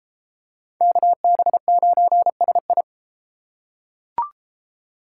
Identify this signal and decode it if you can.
Morse code, heard as a keyed tone, about 33 wpm, 700 Hz. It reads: KB9SI